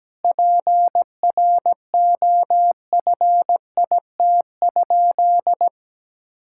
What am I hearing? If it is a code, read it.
Morse code, heard as a keyed tone, about 17 wpm, 700 Hz. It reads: PROFIT?
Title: Sentences from Top 1000 Words - Review Part 5 - 17wpm